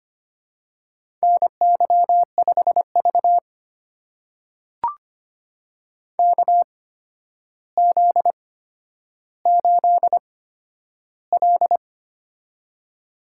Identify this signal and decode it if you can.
Morse code, heard as a keyed tone, about 25 wpm, spaced as for 11 wpm, 700 Hz. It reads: NY5V KZ8L
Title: USA Call Signs - 4 Characters - 8x Character Spacing - 25wpm